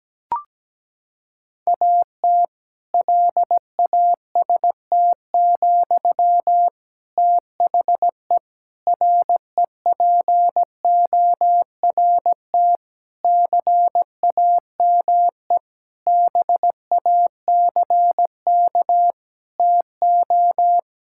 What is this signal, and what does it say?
Morse code, heard as a keyed tone, about 17 wpm, 700 Hz. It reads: AT LAST, THE REPORT CAME BACK TO
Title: Sentences from Top 1000 Words - Review Part 6 - 17wpm